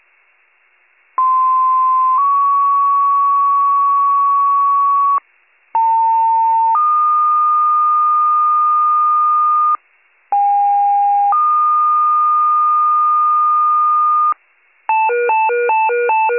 Two-tone page, heard as a one-second tone, then a three-second tone, about 1040 then 1120 Hz, about 900 then 1220 Hz, about 810 then 1220 Hz. It is heard from 1.2 to 14.3 s.